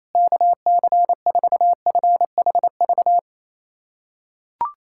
Morse code, heard as a keyed tone, about 28 wpm, 700 Hz. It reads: KC4FHV
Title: USA Callsign DE Callsign - 6 Characters - 28wpm